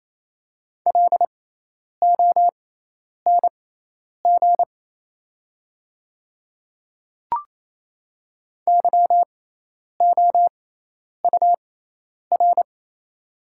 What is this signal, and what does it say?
Morse code, heard as a keyed tone, about 28 wpm, spaced as for 10 wpm, 700 Hz. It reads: LONG YOUR
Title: 6x Character Spacing - Top 100 Words - 28wpm